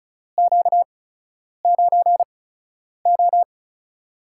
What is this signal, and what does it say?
Morse code, heard as a keyed tone, about 36 wpm, 700 Hz. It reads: Q 9 O
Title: USA Call Signs - 4 Characters - 8x Character Spacing - 35wpm